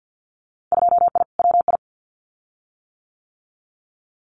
Morse code, heard as a keyed tone, about 50 wpm, 700 Hz. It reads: 2I WI
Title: Winter Field Day Exchanges 50wpm